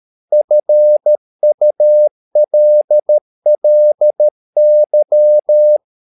Morse code, heard as a keyed tone, about 13 wpm, 600 Hz. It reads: FULLY
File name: mp3 files / Lexiphiles - 13wpm